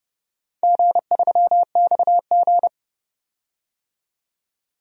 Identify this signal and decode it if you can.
Morse code, heard as a keyed tone, about 30 wpm, 700 Hz. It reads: G3XG
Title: DXCC-100 Call Signs - 4 Characters - 30wpm